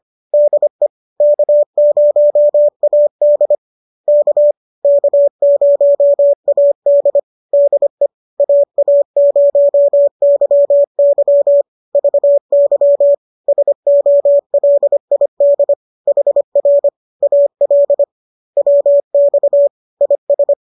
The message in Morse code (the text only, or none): DE K0AD K K0AD DE AA0YY VY SOLID HR AL WX IS